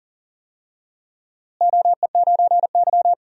Morse code, heard as a keyed tone, about 39 wpm, 700 Hz. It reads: OE9Y
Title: DXCC-100 Call Signs - 4 Characters - Repeated 2x - 40wpm